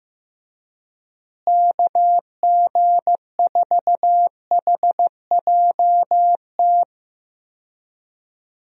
Morse code, heard as a keyed tone, about 14 wpm, 700 Hz. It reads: KG4HJT
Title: USA Call Signs - 6 Characters - Repeated 3x - 15wpm